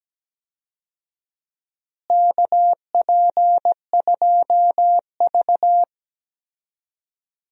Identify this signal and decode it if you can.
Morse code, heard as a keyed tone, about 17 wpm, 700 Hz. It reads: KP2V
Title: DXCC-100 Call Signs - 4 Characters - Repeated 3x - 17wpm